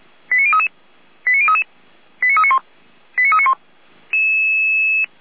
Five-tone selective call (ZVEI), heard as quick runs of five tones, tones of about 70 ms. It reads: 89033 89033 89391 89391